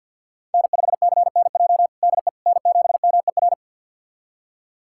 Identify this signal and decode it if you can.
Morse code, heard as a keyed tone, about 53 wpm, 690 Hz. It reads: N5CNJ DE N7MER
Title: USA Callsign DE Callsign - 5 Characters - 50wpm